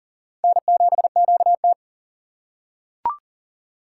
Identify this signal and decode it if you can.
Morse code, heard as a keyed tone, about 40 wpm, 690 Hz. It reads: N7QT
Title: K1USN SST QSO Practice - 40wpm